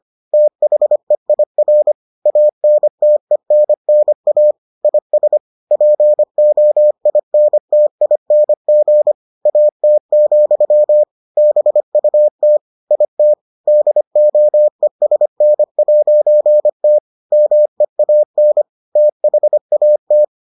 Morse code, heard as a keyed tone, about 25 wpm, 600 Hz. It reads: THEIR ANTENNA IS POINTING AT, BUT IT DOESN'T MEAN THAT